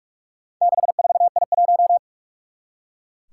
Morse code, heard as a keyed tone, about 44 wpm, 700 Hz. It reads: 64I1
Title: Four Challenging Letter-Number Combinations 45wpm